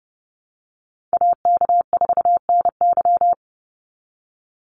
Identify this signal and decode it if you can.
Morse code, heard as a keyed tone, about 30 wpm, 700 Hz. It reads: AK4NY
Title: Call Signs Repeated 2x - 30wpm